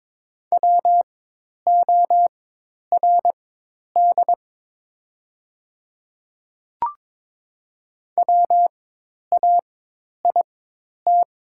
Morse code, heard as a keyed tone, about 22 wpm, spaced as for 10 wpm, 700 Hz. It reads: WORD WAIT
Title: 4x Character Spacing - Top 500 Words - 22wpm